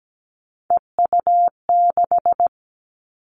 Morse code, heard as a keyed tone, about 17 wpm, 700 Hz. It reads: EU6
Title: Callsign Prefixes for Top 50 DXCC - 17wpm